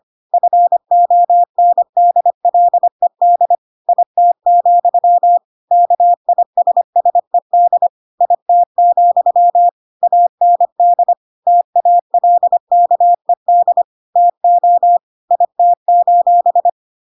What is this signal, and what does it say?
Morse code, heard as a keyed tone, about 25 wpm, 700 Hz. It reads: FONDLED IT, KISSED IT, AND TALKED TO IT:
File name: Pinocchio / Pinocchio ch5 25 25 ik1hge